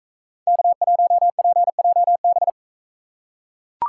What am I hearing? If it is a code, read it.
Morse code, heard as a keyed tone, about 42 wpm, 700 Hz. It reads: K1PJB